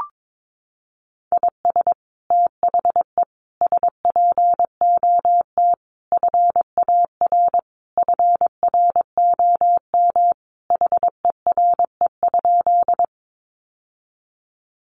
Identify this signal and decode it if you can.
Morse code, heard as a keyed tone, about 22 wpm, 700 Hz. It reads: IS THE SPOT FAR FROM HERE?
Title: Sentences from Top 1100 Words 22wpm